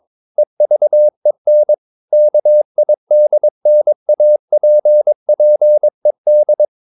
Morse code, heard as a keyed tone, about 22 wpm, 600 Hz. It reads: EVEN KIDNAPPED
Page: Romanen Never af Ken Follet på engelsk.